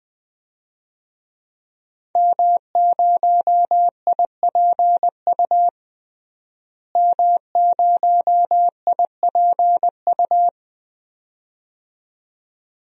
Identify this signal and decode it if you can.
Morse code, heard as a keyed tone, about 20 wpm, 700 Hz. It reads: M0IPU M0IPU
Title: DXCC-100 Call Signs - 5 Characters - Repeated 2x - 20wpm